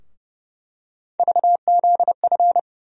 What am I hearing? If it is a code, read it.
Morse code, heard as a keyed tone, about 30 wpm, 700 Hz. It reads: VZF
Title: USA Call Sign Postfixes - 3 Letters 30wpm